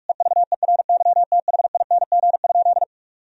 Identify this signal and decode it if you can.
Morse code, heard as a keyed tone, about 41 wpm, 700 Hz. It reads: EVERYTHING?
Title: Sentences from Top 1000 Words - Review Part 2 - 45wpm